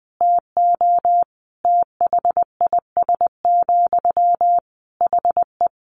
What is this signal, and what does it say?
Morse code, heard as a keyed tone, about 20 wpm, 700 Hz. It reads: TO THIS, HE
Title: Sentences from Top 400 Words 20wpm